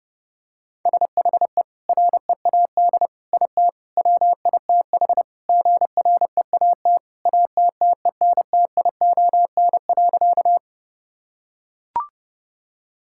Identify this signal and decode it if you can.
Morse code, heard as a keyed tone, about 30 wpm, 700 Hz. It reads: SHE READ IT WITH GREAT ATTENTION.
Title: Sentences from Top 1000 Words - Review Part 6 - 30wpm